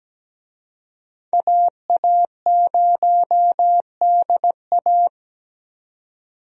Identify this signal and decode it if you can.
Morse code, heard as a keyed tone, about 17 wpm, 700 Hz. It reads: AA0DA
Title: USA Call Signs - 5 Characters - 17wpm